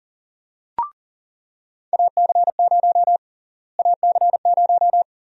Morse code, heard as a keyed tone, about 40 wpm, 700 Hz. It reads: AC0 AC0
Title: USA Call Sign Prefixes - Repeated 2x - 40wpm